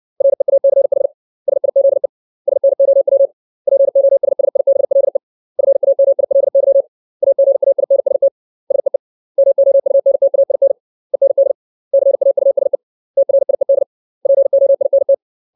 Morse code, heard as a keyed tone, about 60 wpm, 550 Hz. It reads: REACH SIZE SNOW POSSIBLE FAMILY AGAINST SEE MOUNTAIN END CAUSE TRIED POINT
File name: 632 words / 632 Words 60WPM